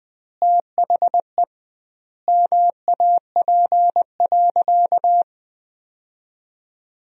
Morse code, heard as a keyed tone, about 20 wpm, 700 Hz. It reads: THE MAP.
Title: 2x Word-Spacing - Sentences from Top 400 Words - 20wpm